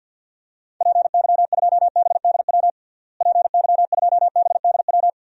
Morse code, heard as a keyed tone, about 51 wpm, 700 Hz. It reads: PY2BDW PY2BDW
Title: DXCC-100 Call Signs - 6 Characters - Repeated 3x - 50wpm